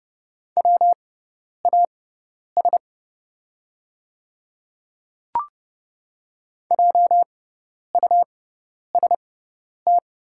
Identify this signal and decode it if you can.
Morse code, heard as a keyed tone, about 29 wpm, spaced as for 10 wpm, 690 Hz. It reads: WAS JUST
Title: 6x Character Spacing - Top 300 Words - 30wpm